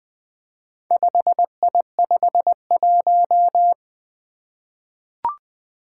Morse code, heard as a keyed tone, about 20 wpm, 700 Hz. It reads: 5I51
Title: Four Challenging Letter-Number Combinations 20wpm